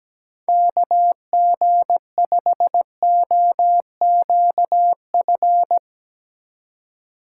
Morse code, heard as a keyed tone, about 17 wpm, 700 Hz. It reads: KG5OQF
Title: USA Call Signs - 6 Characters - 4x Character Spacing - 17wpm